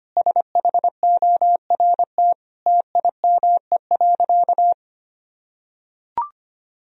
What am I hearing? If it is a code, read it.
Morse code, heard as a keyed tone, about 25 wpm, 700 Hz. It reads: SHORT TIME.
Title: Sentences from Top 1000 Words - Review Part 3 - 25wpm